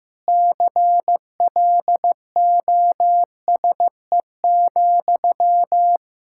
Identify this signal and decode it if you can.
Morse code, heard as a keyed tone, about 15 wpm, 700 Hz. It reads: CLOSE,
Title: Sentences from Top 1600 Words 15wpm